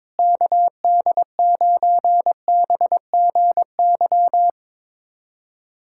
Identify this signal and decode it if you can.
Morse code, heard as a keyed tone, about 22 wpm, 700 Hz. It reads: KD9BGY
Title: USA Callsign DE Callsign - 6 Characters - 22wpm